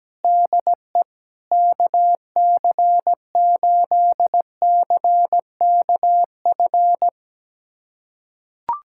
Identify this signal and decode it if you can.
Morse code, heard as a keyed tone, about 17 wpm, 700 Hz. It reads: DE KC8CKF
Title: USA Callsign DE Callsign - 6 Characters - 17wpm